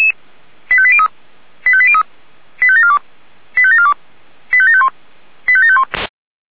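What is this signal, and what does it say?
Five-tone selective call (ZVEI), heard as quick runs of five tones, tones of about 70 ms. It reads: 85803 85803 85742 85742 85751 85751
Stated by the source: ZVEI five-tone selective call received with an SDR.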